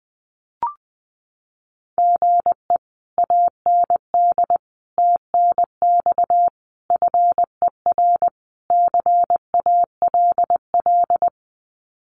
Morse code, heard as a keyed tone, about 20 wpm, 700 Hz. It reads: GE AND TNX FER CALL
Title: Common QSO Phrases Encore 20wpm